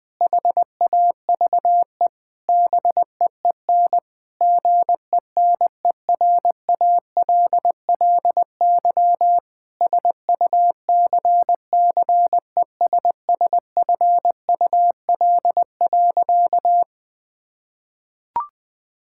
Morse code, heard as a keyed tone, about 19 wpm, 700 Hz. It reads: HAVE BEEN GENERALLY SUCCESSFUL.
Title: Sentences from Top 1000 Words - Review Part 6 - 20wpm